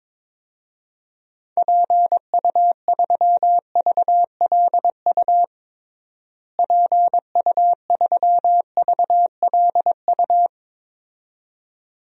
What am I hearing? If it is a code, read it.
Morse code, heard as a keyed tone, about 22 wpm, 700 Hz. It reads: PU3VLU PU3VLU
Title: DXCC-100 Call Signs - 6 Characters - Repeated 2x - 22wpm